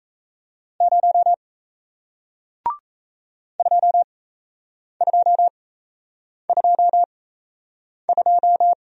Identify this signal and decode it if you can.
Morse code, heard as a keyed tone, about 37 wpm, spaced as for 8 wpm, 700 Hz. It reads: 0 2222